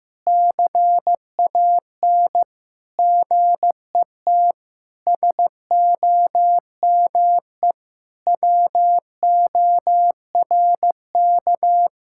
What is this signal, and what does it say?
Morse code, heard as a keyed tone, about 15 wpm, 700 Hz. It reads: CAN GET SOME WORK
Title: Sentences from Top 1000 Words - Review Part 6 - 15wpm